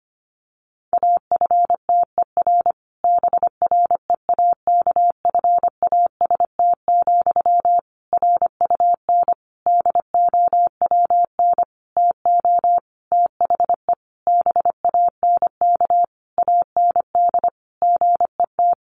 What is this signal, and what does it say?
Morse code, heard as a keyed tone, about 25 wpm, 700 Hz. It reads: AFTER BREAKFAST, RUN DOWN TO THE BANK AND GET
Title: Sentences from Top 1500 Words 25wpm